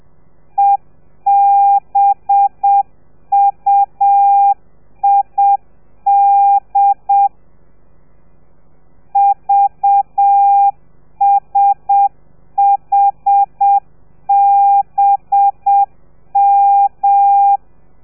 Morse code, heard as a keyed tone, about 7 wpm, 800 Hz. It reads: EBUID VSHBM
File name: telegrafia / i2tfj / Lez4 30